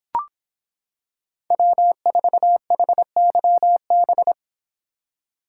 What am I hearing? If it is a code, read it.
Morse code, heard as a keyed tone, about 26 wpm, 700 Hz. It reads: W4HYB